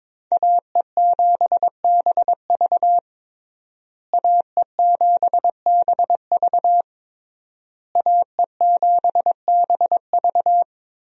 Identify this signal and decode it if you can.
Morse code, heard as a keyed tone, about 21 wpm, 700 Hz. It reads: AE7BV AE7BV AE7BV
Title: USA Call Signs - 5 Characters - Repeated 3x - 22wpm